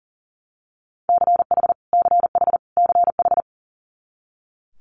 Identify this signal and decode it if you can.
Morse code, heard as a keyed tone, about 42 wpm, 700 Hz. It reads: CH CH CH
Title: Multi-letter Phonemes Repeated 3x 40wpm